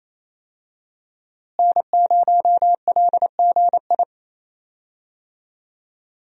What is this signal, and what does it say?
Morse code, heard as a keyed tone, about 28 wpm, 700 Hz. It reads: N0LGI
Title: USA Call Signs - 5 Characters - Repeated 2x - 28wpm